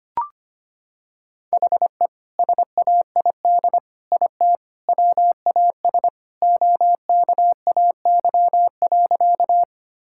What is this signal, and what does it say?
Morse code, heard as a keyed tone, about 25 wpm, 700 Hz. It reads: HE SAID IT WAS OKAY.
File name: practice / 25wpm / Sentences with 2-Word Phrases - 25wpm